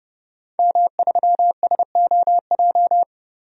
Morse code, heard as a keyed tone, about 32 wpm, 700 Hz. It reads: M3SOJ
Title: DXCC-100 Call Signs - 5 Characters - 30wpm